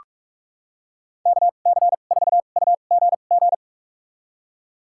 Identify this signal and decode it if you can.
Morse code, heard as a keyed tone, about 46 wpm, spaced as for 32 wpm, 690 Hz. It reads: KC4UGG
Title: USA Call Signs - 6 Characters - 2x Character Spacing - 45wpm